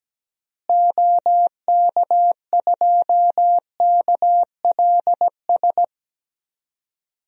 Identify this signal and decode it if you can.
Morse code, heard as a keyed tone, about 17 wpm, 700 Hz. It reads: OK2KLS
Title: DXCC-100 Call Signs - 6 Characters - Repeated 3x - 17wpm